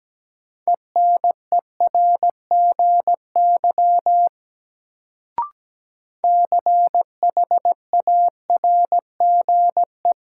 Morse code, heard as a keyed tone, about 17 wpm, 700 Hz. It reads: ENERGY CHARGE
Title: Sentences from Top 900 Words - New Word Drill - 17wpm